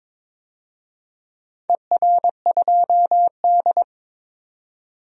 Morse code, heard as a keyed tone, about 21 wpm, 700 Hz. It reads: ER2D
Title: DXCC-100 Call Signs - 4 Characters - 22wpm